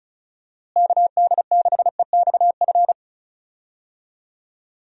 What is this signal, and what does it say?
Morse code, heard as a keyed tone, about 36 wpm, 700 Hz. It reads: KD6EXF
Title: USA Call Signs - 6 Characters - Repeated 3x - 35wpm